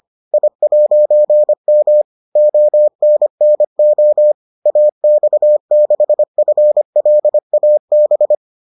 Morse code, heard as a keyed tone, about 25 wpm, 600 Hz. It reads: I'M ONNO AX6FLAB